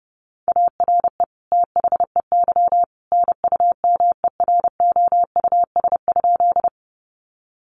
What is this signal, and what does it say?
Morse code, heard as a keyed tone, about 30 wpm, 700 Hz. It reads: ARE THEY NUMEROUS?
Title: Sentences from Top 1400 Words 30wpm